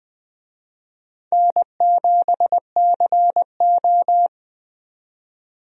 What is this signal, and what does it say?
Morse code, heard as a keyed tone, about 20 wpm, 700 Hz. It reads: N7CO
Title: USA Call Signs - 4 Characters - 20wpm